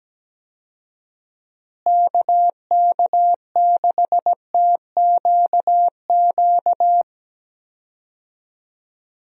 Morse code, heard as a keyed tone, about 17 wpm, 700 Hz. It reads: KK6TQQ
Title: USA Call Signs - 6 Characters - Repeated 3x - 17wpm